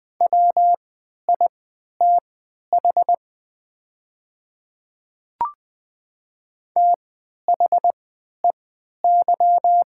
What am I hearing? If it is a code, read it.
Morse code, heard as a keyed tone, about 20 wpm, spaced as for 11 wpm, 700 Hz. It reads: WITH THEY
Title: 3x Character Spacing - Top 200 Words - 20wpm